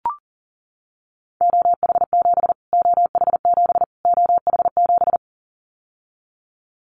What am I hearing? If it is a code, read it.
Morse code, heard as a keyed tone, about 40 wpm, 700 Hz. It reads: OH7 OH7 OH7
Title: Callsign Prefixes for Top 25 DXCC - Repeated 3x - 40wpm